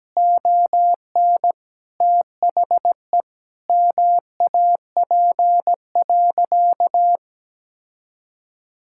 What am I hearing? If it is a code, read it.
Morse code, heard as a keyed tone, about 17 wpm, 700 Hz. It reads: ON THE MAP.
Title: Sentences from Top 1000 Words - Review Part 6 - 17wpm